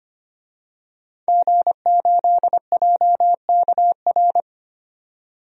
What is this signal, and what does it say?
Morse code, heard as a keyed tone, about 25 wpm, 700 Hz. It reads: G8JKR
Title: DXCC-100 Call Signs - 5 Characters - 25wpm